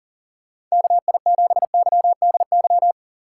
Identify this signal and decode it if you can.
Morse code, heard as a keyed tone, about 38 wpm, 700 Hz. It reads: KI7YDY
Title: USA Call Signs - 6 Characters - 4x Character Spacing - 40wpm